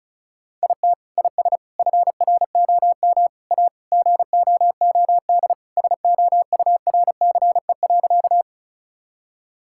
Morse code, heard as a keyed tone, about 34 wpm, 700 Hz. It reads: IT IS FROM A GOOD SOURCE.
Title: Sentences from Top 1000 Words - Review Part 5 - 35wpm